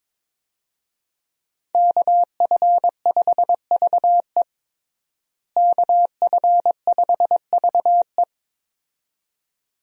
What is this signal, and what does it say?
Morse code, heard as a keyed tone, about 22 wpm, 700 Hz. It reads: KF5VE KF5VE
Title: USA Call Signs - 5 Characters - Repeated 2x - 22wpm